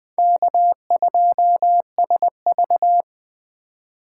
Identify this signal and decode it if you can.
Morse code, heard as a keyed tone, about 20 wpm, 700 Hz. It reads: K2SV